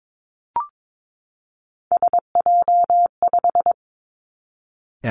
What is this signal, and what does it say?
Morse code, heard as a keyed tone, about 22 wpm, 700 Hz. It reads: SJ5
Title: Three Challenging Letter-Number Combinations 22wpm